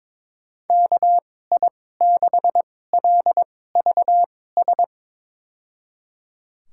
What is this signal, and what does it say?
Morse code, heard as a keyed tone, about 22 wpm, spaced as for 16 wpm, 700 Hz. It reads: KI6LVS
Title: USA Call Signs - 6 Characters - 2x Character Spacing - 22wpm